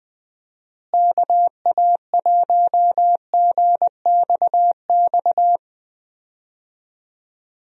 Morse code, heard as a keyed tone, about 21 wpm, 700 Hz. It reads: KA1GXX
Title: USA Call Signs - 6 Characters - Repeated 2x - 20wpm